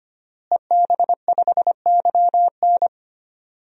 Morse code, heard as a keyed tone, about 25 wpm, 700 Hz. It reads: EB5YN
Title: DXCC-100 Call Signs - 5 Characters - Repeated 2x - 25wpm